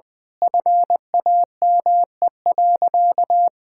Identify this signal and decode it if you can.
Morse code, heard as a keyed tone, about 20 wpm, 700 Hz. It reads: FAME.
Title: Sentences with 6-Word Phrases - 20wpm